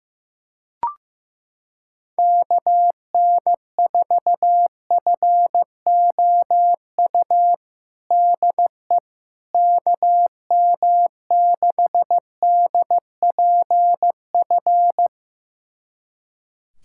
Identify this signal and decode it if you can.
Morse code, heard as a keyed tone, about 15 wpm, 700 Hz. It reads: KN4FOU DE KM6DPF
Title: USA Callsign DE Callsign - 6 Characters - 15wpm